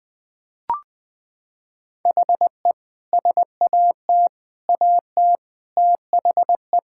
Morse code, heard as a keyed tone, about 21 wpm, 700 Hz. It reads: HE SAT AT THE
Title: Sentences with 5-Word Phrases - 20wpm